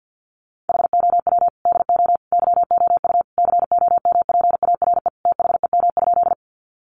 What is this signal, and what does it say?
Morse code, heard as a keyed tone, about 50 wpm, 700 Hz. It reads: HOW DO YOU COMPARE THEM?
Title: Sentences from Top 1300 Words 50wpm